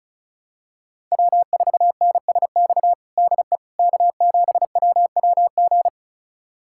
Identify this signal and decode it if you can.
Morse code, heard as a keyed tone, about 34 wpm, 700 Hz. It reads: W4NSX DE K7WWG
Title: USA Callsign DE Callsign 35wpm